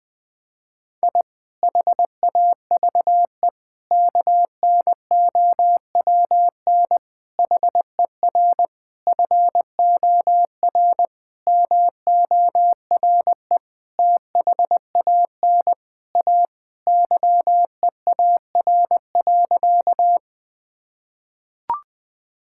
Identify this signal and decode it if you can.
Morse code, heard as a keyed tone, about 20 wpm, 700 Hz. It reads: I HAVE KNOWN HER FOR MORE THAN A YEAR.
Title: Sentences with 5-Word Phrases - Spoken Beginning - 20wpm